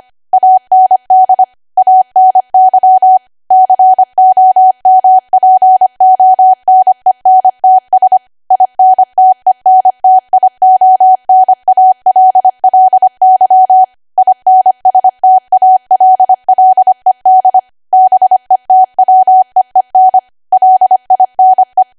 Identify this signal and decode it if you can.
Morse code, heard as a keyed tone, about 25 wpm, 750 Hz. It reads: AND ANY COMPONENTS INTENTIONALLY INSTALLED BETWEEN LINE